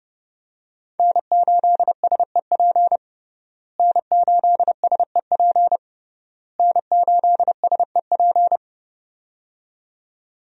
Morse code, heard as a keyed tone, about 29 wpm, 700 Hz. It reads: N8SEP N8SEP N8SEP
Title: USA Call Signs - 5 Characters - Repeated 3x - 30wpm